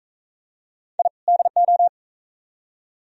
Morse code, heard as a keyed tone, about 42 wpm, 700 Hz. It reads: I DO